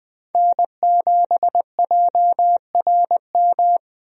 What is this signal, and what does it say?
Morse code, heard as a keyed tone, about 21 wpm, 700 Hz. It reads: N7JRM
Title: USA Call Signs - 5 Characters - Repeated 2x - 20wpm